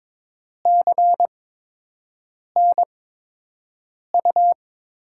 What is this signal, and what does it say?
Morse code, heard as a keyed tone, about 22 wpm, 700 Hz. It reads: C N U
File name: Top 90 QSO Elements - 8x Character Spacing - 22wpm